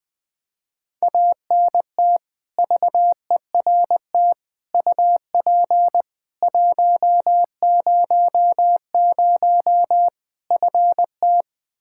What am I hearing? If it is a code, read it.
Morse code, heard as a keyed tone, about 20 wpm, 700 Hz. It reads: ANT VERT UP 100 FT
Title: Practice QSO - Exchange 2 - Info - 20wpm